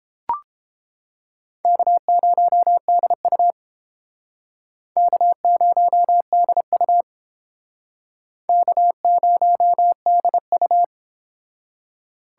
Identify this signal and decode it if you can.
Morse code, heard as a keyed tone, about 31 wpm, 700 Hz. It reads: K0DU K0DU K0DU